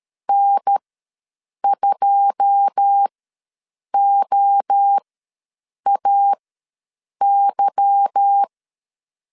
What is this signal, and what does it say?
Morse code, heard as a keyed tone, about 13 wpm, spaced as for 7 wpm, 800 Hz. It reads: N2OAY
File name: LEZIONI / 9 28 ws 12wpm 3354